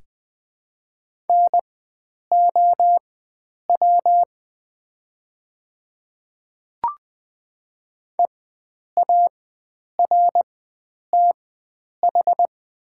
Morse code, heard as a keyed tone, about 20 wpm, spaced as for 9 wpm, 700 Hz. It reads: NOW EARTH
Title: 4x Character Spacing - Top 300 Words - 20wpm